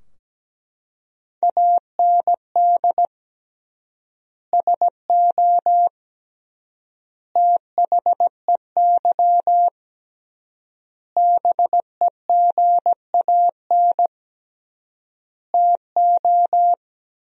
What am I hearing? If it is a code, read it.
Morse code, heard as a keyed tone, about 17 wpm, 700 Hz. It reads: AND SO THEY BEGAN TO
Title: 3x Word-Spacing - Sentences from Top 400 Words - 17wpm